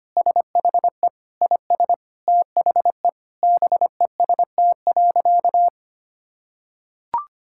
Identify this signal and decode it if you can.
Morse code, heard as a keyed tone, about 25 wpm, 700 Hz. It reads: SHE IS THE BEST.
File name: practice / 25wpm / Sentences with 4-Word Phrases - 25wpm